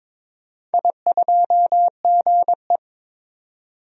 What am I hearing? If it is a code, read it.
Morse code, heard as a keyed tone, about 22 wpm, 700 Hz. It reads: I2GE
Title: DXCC-100 Call Signs - 4 Characters - 22wpm